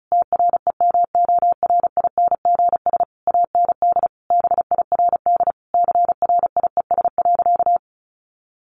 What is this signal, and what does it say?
Morse code, heard as a keyed tone, about 34 wpm, 700 Hz. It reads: TREMORINGS AND BIRD CRIES.
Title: A Sound Of Thunder 35wpm